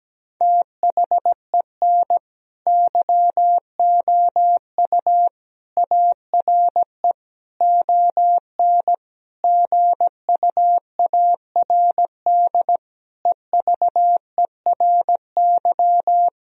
Morse code, heard as a keyed tone, about 17 wpm, 700 Hz. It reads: THEN YOU ARE ON GUARD EVERY
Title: Sentences from Top 1000 Words - Review Part 4 - 17wpm